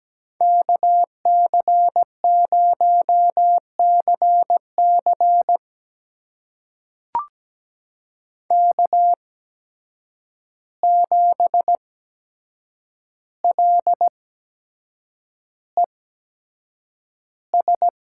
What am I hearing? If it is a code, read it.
Morse code, heard as a keyed tone, about 17 wpm, spaced as for 7 wpm, 700 Hz. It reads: KC0CC K7LES
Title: USA Call Signs - 5 Characters - 8x Character Spacing - 17wpm